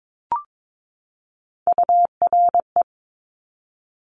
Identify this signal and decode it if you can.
Morse code, heard as a keyed tone, about 22 wpm, 700 Hz. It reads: URE
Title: Top Three Letter Combinations in Words 22wpm